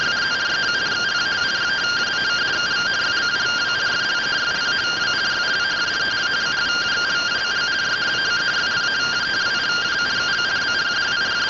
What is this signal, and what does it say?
Two-tone teleprinter signal, about 170 Hz shift.